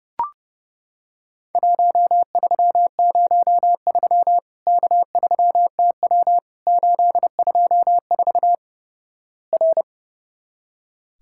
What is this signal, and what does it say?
Morse code, heard as a keyed tone, about 32 wpm, 700 Hz. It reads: 1303 K3TW 824 R
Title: Worked All Europe DX Contest - QTC Exchange - 30wpm